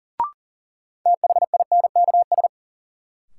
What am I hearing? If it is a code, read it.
Morse code, heard as a keyed tone, about 40 wpm, 700 Hz. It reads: THINKS
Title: Sentences from Top 1000 Words - Review Part 6 - New Word Drill - 40wpm